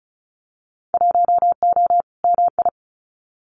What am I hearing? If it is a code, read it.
Morse code, heard as a keyed tone, about 36 wpm, 700 Hz. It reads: 1O MI
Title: Winter Field Day Exchanges Encore 35wpm